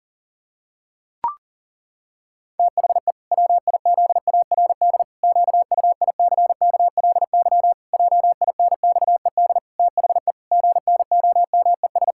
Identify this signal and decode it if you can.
Morse code, heard as a keyed tone, about 40 wpm, 700 Hz. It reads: THE WIZARD QUICKLY JINXED THE GNOMES
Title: Pangram Challenge v2 - 40wpm